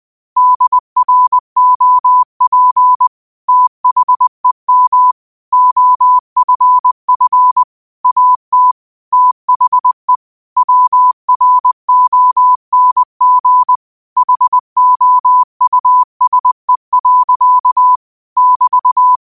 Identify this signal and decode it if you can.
Morse code, heard as a keyed tone, about 20 wpm, 1000 Hz. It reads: DROP THEM OFF AT THE WRONG HOUSE. <BT>
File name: qotdfiles / 20wpm